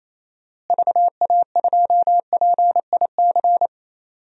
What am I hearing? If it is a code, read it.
Morse code, heard as a keyed tone, about 28 wpm, 700 Hz. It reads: VA2PIC
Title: DXCC-100 Call Signs - 6 Characters - Repeated 3x - 28wpm